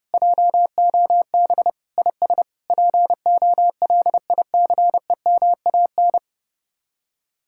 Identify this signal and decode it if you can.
Morse code, heard as a keyed tone, about 38 wpm, 700 Hz. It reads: JOB IS POLICEMAN
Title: Speed-Racing Common QSO Phrases Encore - 30wpm